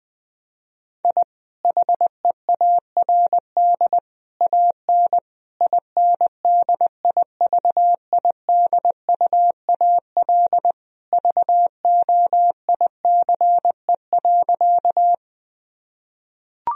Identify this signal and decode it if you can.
Morse code, heard as a keyed tone, about 21 wpm, 700 Hz. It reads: I HEARD AN INDIVIDUAL VOICE.
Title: Sentences from Top 1000 Words - Review Part 4 - 20wpm